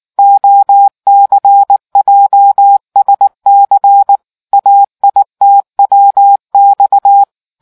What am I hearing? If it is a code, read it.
Morse code, heard as a keyed tone, about 19 wpm, 800 Hz. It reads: OCJSC AITWX